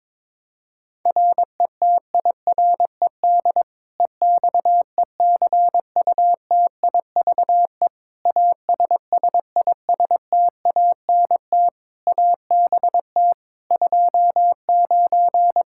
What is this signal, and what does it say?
Morse code, heard as a keyed tone, about 22 wpm, 700 Hz. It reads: RETIRED EXECUTIVE ASSISTANT ABT 29
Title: Practice QSO - Exchange 3 - More Info - 22wpm